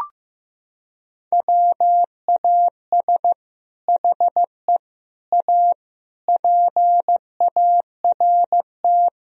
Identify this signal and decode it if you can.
Morse code, heard as a keyed tone, about 15 wpm, 700 Hz. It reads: WAS HE A PART
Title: Sentences from Top 1000 Words - Review Part 4 - 15wpm